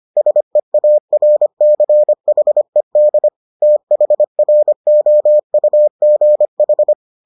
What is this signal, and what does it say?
Morse code, heard as a keyed tone, about 25 wpm, 600 Hz. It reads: SEARCHED THROUGH